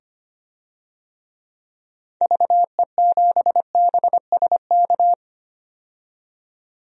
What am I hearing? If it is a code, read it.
Morse code, heard as a keyed tone, about 25 wpm, 700 Hz. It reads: VE7BSK
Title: DXCC-100 Call Signs - 6 Characters - 25wpm